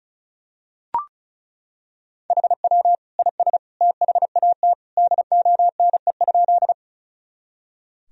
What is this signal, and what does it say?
Morse code, heard as a keyed tone, about 36 wpm, 700 Hz. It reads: HW IS THAT DONE?
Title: Sentences from Top 500 Words and Abbreviations 35wpm